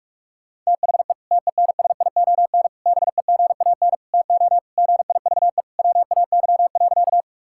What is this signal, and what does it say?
Morse code, heard as a keyed tone, about 44 wpm, 700 Hz. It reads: THE TENSION BEGAN TO GIVE WAY.
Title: Sentences from Top 1600 Words 45wpm